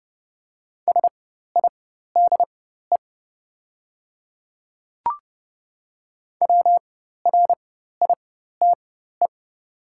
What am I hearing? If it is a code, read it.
Morse code, heard as a keyed tone, about 29 wpm, spaced as for 14 wpm, 700 Hz. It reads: SIDE WRITE
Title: 4x Character Spacing - Top 100 Words - 30wpm